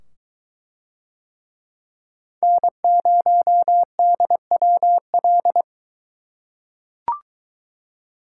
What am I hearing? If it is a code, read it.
Morse code, heard as a keyed tone, about 21 wpm, 700 Hz. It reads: N0DWL